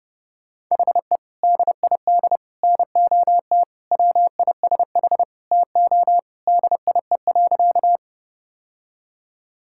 Morse code, heard as a keyed tone, about 32 wpm, 700 Hz. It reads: HE DID NOT WISH TO DIE.
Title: Sentences from Top 1600 Words 30wpm